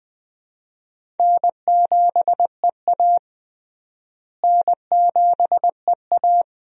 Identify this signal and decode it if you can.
Morse code, heard as a keyed tone, about 20 wpm, 700 Hz. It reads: N7EA N7EA
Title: USA Call Signs - 4 Characters - Repeated 2x - 20wpm